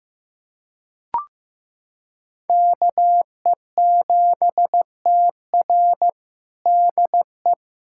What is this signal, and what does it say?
Morse code, heard as a keyed tone, about 15 wpm, 700 Hz. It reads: KE7TR DE
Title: USA Callsign DE Callsign - 5 Characters - 15wpm